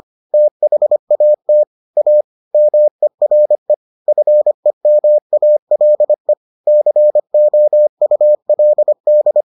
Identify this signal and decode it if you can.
Morse code, heard as a keyed tone, about 25 wpm, 600 Hz. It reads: THAT A MERE FEMALE COULD